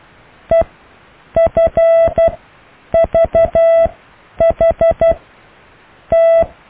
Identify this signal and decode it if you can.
Morse code, heard as a keyed tone, about 12 wpm, spaced as for 8 wpm, 660 Hz. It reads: EFVHT
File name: cw / EKTPV+BFHNW